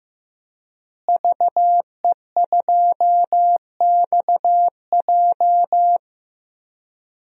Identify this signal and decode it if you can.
Morse code, heard as a keyed tone, about 15 wpm, 700 Hz. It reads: VE2XJ
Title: DXCC-100 Call Signs - 5 Characters - Repeated 3x - 15wpm